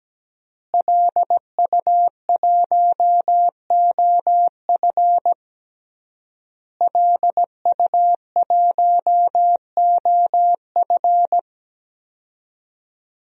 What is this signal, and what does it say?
Morse code, heard as a keyed tone, about 17 wpm, 700 Hz. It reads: LU1OF LU1OF
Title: DXCC-100 Call Signs - 5 Characters - Repeated 3x - 17wpm